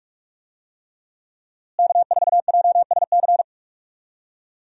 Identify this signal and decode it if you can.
Morse code, heard as a keyed tone, about 46 wpm, 700 Hz. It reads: K4JSC
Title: USA Call Signs - 5 Characters - Repeated 3x - 45wpm